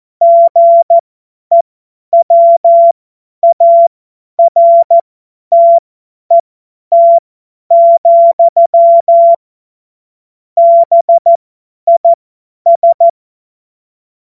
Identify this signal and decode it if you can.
Morse code, heard as a keyed tone, about 14 wpm, spaced as for 10 wpm, 680 Hz. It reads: GEWARTET, BIS
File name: Morsekurs / Witze / Witz 28. Dezember 2021, 50 F70